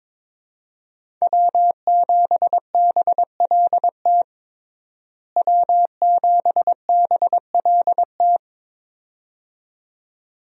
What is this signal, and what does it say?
Morse code, heard as a keyed tone, about 22 wpm, 700 Hz. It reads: W7BLT W7BLT
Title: USA Call Signs - 5 Characters - Repeated 2x - 22wpm